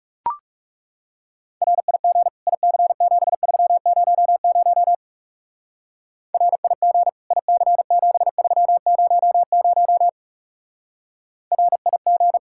Morse code, heard as a keyed tone, about 36 wpm, 700 Hz. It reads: RIG IC7300 RIG IC7300 RIG
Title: Speed-Racing Common QSO Phrases - 30wpm